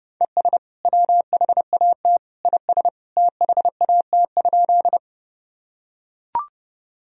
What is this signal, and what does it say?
Morse code, heard as a keyed tone, about 30 wpm, 700 Hz. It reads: ES WHAT IS THAT?
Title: Sentences from Top 100 Words and Abbreviations 30wpm